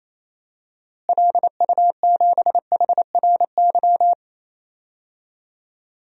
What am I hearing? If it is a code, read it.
Morse code, heard as a keyed tone, about 28 wpm, 700 Hz. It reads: LU7HRY
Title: DXCC-100 Call Signs - 6 Characters - Repeated 3x - 28wpm